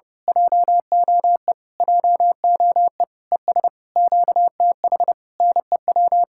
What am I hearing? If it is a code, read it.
Morse code, heard as a keyed tone, about 28 wpm, 700 Hz. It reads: JOE JOE ES QTH NEW
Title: Speed-Racing Common QSO Phrases Encore - 30wpm